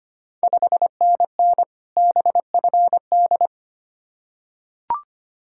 Morse code, heard as a keyed tone, about 25 wpm, 690 Hz. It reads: 5NN BFD
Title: Ontario QSO Party Exchange 25wpm